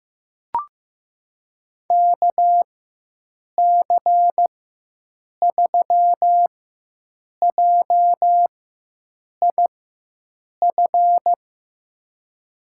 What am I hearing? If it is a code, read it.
Morse code, heard as a keyed tone, about 15 wpm, spaced as for 7 wpm, 700 Hz. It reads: KC3JIF
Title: USA Call Signs - 6 Characters - 4x Character Spacing - 15wpm